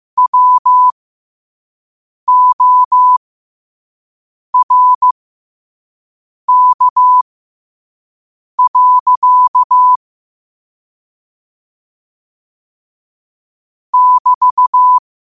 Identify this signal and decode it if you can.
Morse code, heard as a keyed tone, about 15 wpm, spaced as for 5 wpm, 1000 Hz. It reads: WORK. <BT>
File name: qotdfiles / 5wpm